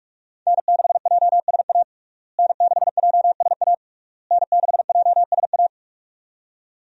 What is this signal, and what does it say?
Morse code, heard as a keyed tone, about 41 wpm, 700 Hz. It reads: N6JSA N6JSA N6JSA
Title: USA Call Signs - 5 Characters - Repeated 3x - 45wpm